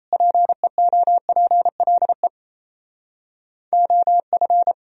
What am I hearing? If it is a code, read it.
Morse code, heard as a keyed tone, about 35 wpm, 700 Hz. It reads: PEOPLE OF
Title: Speed-Racing - Sets of 3 Words 28wpm